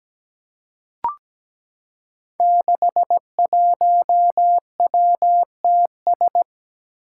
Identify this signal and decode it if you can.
Morse code, heard as a keyed tone, about 17 wpm, 700 Hz. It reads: 61WTS
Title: Five Challenging Letter-Number Combinations 17wpm